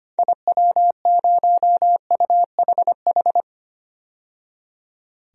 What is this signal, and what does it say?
Morse code, heard as a keyed tone, about 25 wpm, 700 Hz. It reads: IW0UHH
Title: DXCC-100 Call Signs - 6 Characters - 25wpm